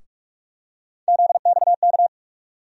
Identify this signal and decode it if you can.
Morse code, heard as a keyed tone, about 43 wpm, 700 Hz. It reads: ZXK
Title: USA Call Sign Postfixes - 3 Letters 45wpm